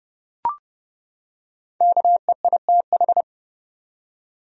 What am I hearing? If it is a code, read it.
Morse code, heard as a keyed tone, about 30 wpm, 700 Hz. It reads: KEITH
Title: USA Common Names - Max 5 Letters - 30wpm